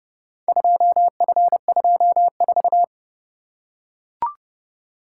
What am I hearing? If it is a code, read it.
Morse code, heard as a keyed tone, about 28 wpm, 700 Hz. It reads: 2F24